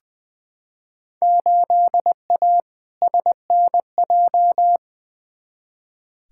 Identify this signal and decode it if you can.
Morse code, heard as a keyed tone, about 21 wpm, 700 Hz. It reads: 8A SNJ
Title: Field Day Exchanges Encore - 20wpm